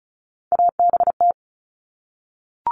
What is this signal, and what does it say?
Morse code, heard as a keyed tone, about 34 wpm, 700 Hz. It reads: ABT
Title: Abbreviations Repeated 3x - 35wpm